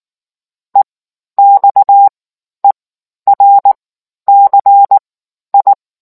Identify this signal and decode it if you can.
Morse code, heard as a keyed tone, about 19 wpm, spaced as for 11 wpm, 800 Hz. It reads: EXERCI